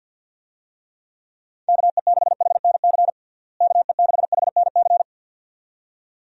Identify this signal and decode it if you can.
Morse code, heard as a keyed tone, about 51 wpm, 700 Hz. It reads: KE6HNC KE6HNC
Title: USA Call Signs - 6 Characters - Repeated 2x - 50wpm